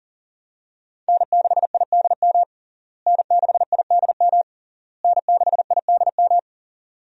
Morse code, heard as a keyed tone, about 40 wpm, 700 Hz. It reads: N6IDM N6IDM N6IDM
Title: USA Call Signs - 5 Characters - Repeated 3x - 40wpm